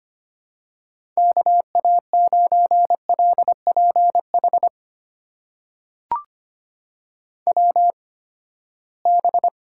Morse code, heard as a keyed tone, about 25 wpm, spaced as for 17 wpm, 700 Hz. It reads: KA9LPH WB
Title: USA Call Signs - 6 Characters - 8x Character Spacing - 25wpm